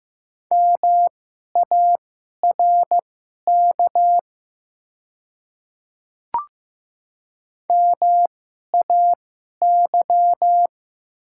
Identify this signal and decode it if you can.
Morse code, heard as a keyed tone, about 15 wpm, spaced as for 11 wpm, 700 Hz. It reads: MARK MAY
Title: 2x Character Spacing - Top 300 Words - 15wpm